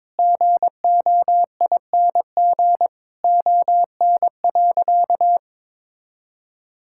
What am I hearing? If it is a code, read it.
Morse code, heard as a keyed tone, about 21 wpm, 700 Hz. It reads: GOING ON.
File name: Sentences with 7-Word Phrases - 22wpm